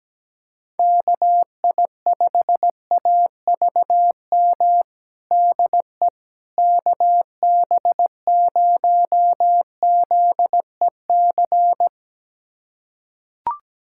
Morse code, heard as a keyed tone, about 17 wpm, 700 Hz. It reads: KI5AVM DE KB0ZEC
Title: USA Callsign DE Callsign - 6 Characters - 17wpm